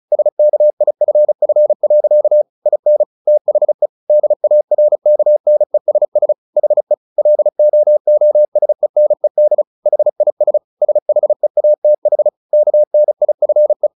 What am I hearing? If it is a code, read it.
Morse code, heard as a keyed tone, about 34 wpm, 600 Hz. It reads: SKIFF. IN THE DARKNESS HE LOOSENED HIS SHEATH KNIFE
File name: the old man and the sea / the old man and the sea 1 35wpm